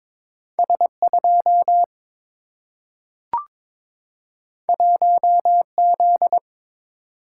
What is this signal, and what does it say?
Morse code, heard as a keyed tone, about 22 wpm, 700 Hz. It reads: S 2 1 Z
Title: Single-Letter Single-Number 22wpm